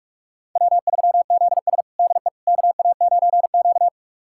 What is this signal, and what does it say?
Morse code, heard as a keyed tone, about 46 wpm, 700 Hz. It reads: W3ZS DE KA9Q
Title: USA Callsign DE Callsign - 4 Characters - 45wpm